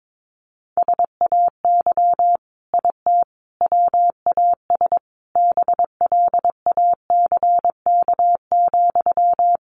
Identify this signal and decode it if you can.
Morse code, heard as a keyed tone, about 22 wpm, 700 Hz. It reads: SAY IT WAS BLACK,
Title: Sentences from Top 800 Words 22wpm